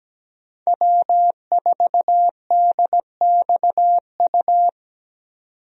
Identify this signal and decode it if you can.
Morse code, heard as a keyed tone, about 17 wpm, 700 Hz. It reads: W4DXU
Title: USA Call Signs - 5 Characters - 17wpm